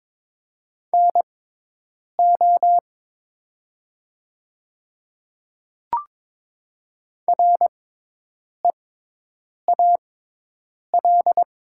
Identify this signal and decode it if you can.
Morse code, heard as a keyed tone, about 22 wpm, spaced as for 8 wpm, 700 Hz. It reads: NO REAL
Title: 6x Character Spacing - Top 300 Words - 22wpm